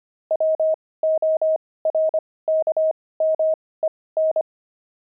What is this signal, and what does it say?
Morse code, heard as a keyed tone, about 25 wpm, spaced as for 18 wpm, 630 Hz. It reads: WORKMEN
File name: pangram-fs-18-cs-25-steaks